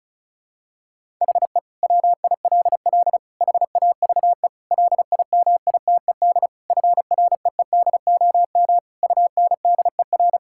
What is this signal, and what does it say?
Morse code, heard as a keyed tone, about 36 wpm, 700 Hz. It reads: HE WILL HAVE LIMITED FREEDOM UNDER
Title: Sentences from Top 1000 Words - Review Part 6 - 35wpm